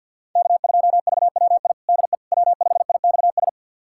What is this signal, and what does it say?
Morse code, heard as a keyed tone, about 52 wpm, 700 Hz. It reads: K2VWI DE W5IXS
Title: USA Callsign DE Callsign - 5 Characters - 50wpm